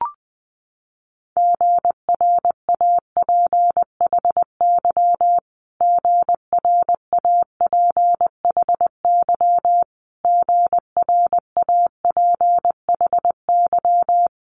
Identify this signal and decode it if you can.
Morse code, heard as a keyed tone, about 20 wpm, 700 Hz. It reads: GRAPHY GRAPHY GRAPHY
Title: Multi-letter Phonemes Repeated 3x 20wpm